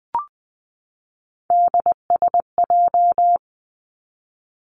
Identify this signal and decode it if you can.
Morse code, heard as a keyed tone, about 20 wpm, 700 Hz. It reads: DSJ
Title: Three Challenging Letter-Number Combinations 20wpm